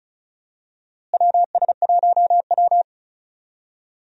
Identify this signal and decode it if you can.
Morse code, heard as a keyed tone, about 34 wpm, 700 Hz. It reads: WS1W
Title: USA Call Signs - 4 Characters - Repeated 2x - 35wpm